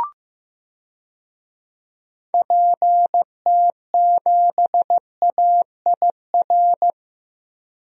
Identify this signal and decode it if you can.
Morse code, heard as a keyed tone, about 15 wpm, 700 Hz. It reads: PT7AIR
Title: DXCC-100 Call Signs - 6 Characters - Repeated 2x - 15wpm